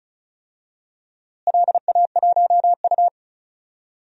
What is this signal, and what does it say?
Morse code, heard as a keyed tone, about 34 wpm, 700 Hz. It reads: LA1U
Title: DXCC-100 Call Signs - 4 Characters - Repeated 3x - 35wpm